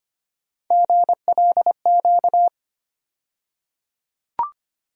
Morse code, heard as a keyed tone, about 25 wpm, 700 Hz. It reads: GLQ
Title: USA Call Sign Postfixes - 3 Letters 25wpm